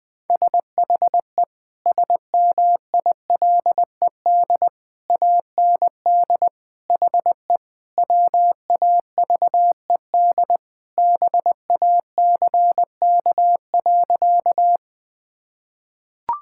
Morse code, heard as a keyed tone, about 20 wpm, 700 Hz. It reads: SHE SMILED AND HE WAVED BACK.
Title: Sentences with 2-Word Phrases - Spoken Beginning - 20wpm